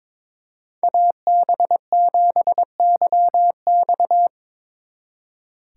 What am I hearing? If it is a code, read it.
Morse code, heard as a keyed tone, about 23 wpm, 700 Hz. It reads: AB7YX
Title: Speed Racing - USA 2x2 Call Signs 22wpm